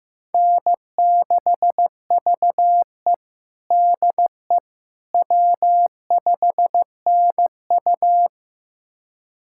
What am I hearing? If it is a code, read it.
Morse code, heard as a keyed tone, about 15 wpm, 700 Hz. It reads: N6VE DE W5NU
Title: USA Callsign DE Callsign - 4 Characters - 15wpm